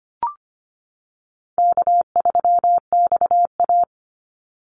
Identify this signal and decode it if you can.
Morse code, heard as a keyed tone, about 25 wpm, 700 Hz. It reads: K3XA
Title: Call Signs - 25wpm